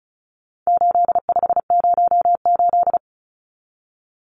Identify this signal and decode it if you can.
Morse code, heard as a keyed tone, about 34 wpm, 700 Hz. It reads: 8508
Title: Numbers - 4-Digit 35wpm